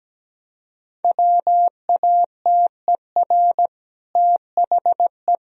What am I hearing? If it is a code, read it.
Morse code, heard as a keyed tone, about 17 wpm, 700 Hz. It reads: WATER THE
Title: Sentences from Top 1000 Words - Review Part 1 - 17wpm